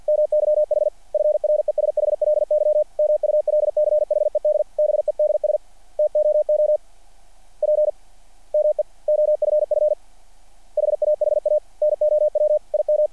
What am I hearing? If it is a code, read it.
Morse code, heard as a keyed tone, about 50 wpm, 600 Hz. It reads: MYH FRESHLY MULCHED BEDS TOO P ME OFF HAHA NOW IM
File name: archived / CRSnet